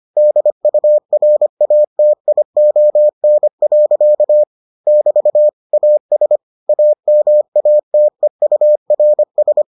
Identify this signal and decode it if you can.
Morse code, heard as a keyed tone, about 25 wpm, 600 Hz. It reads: DURATION. <BT> AS AMATEURS